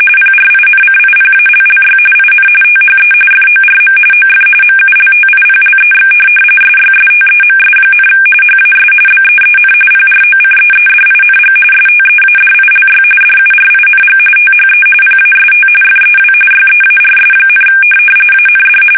RTTY, about 45.45 baud. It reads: PRINCIPALLY IN THE FORM OF MULTIPLEX SYSTEMS, HAS BEEN PROCEEDING STEADILY DURING THE PAST FEW YEARS, AND